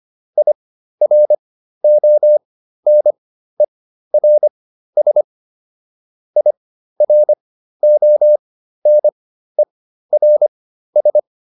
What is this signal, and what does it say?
Morse code, heard as a keyed tone, about 25 wpm, spaced as for 13 wpm, 600 Hz. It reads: IRONERS IRONERS